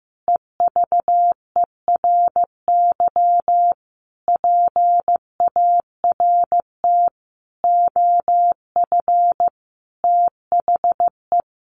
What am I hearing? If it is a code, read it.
Morse code, heard as a keyed tone, about 15 wpm, 700 Hz. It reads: EVERY PART OF THE
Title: Sentences from Top 1500 Words 15wpm